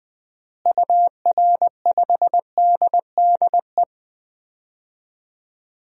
Morse code, heard as a keyed tone, about 21 wpm, 700 Hz. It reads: UR5DDE
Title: DXCC-100 Call Signs - 6 Characters - Repeated 3x - 20wpm